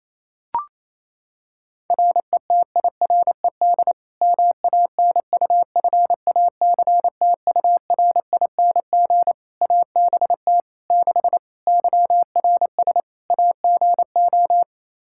Morse code, heard as a keyed tone, about 28 wpm, 700 Hz. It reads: RETIRED MANUFACTURING ABT 6 YRS AGO
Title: Practice QSO - Exchange 3 - More Info - 28wpm